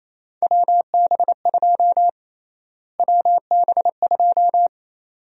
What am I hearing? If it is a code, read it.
Morse code, heard as a keyed tone, about 27 wpm, 700 Hz. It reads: WB2 WB2
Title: USA Call Sign Prefixes - Repeated 2x - 28wpm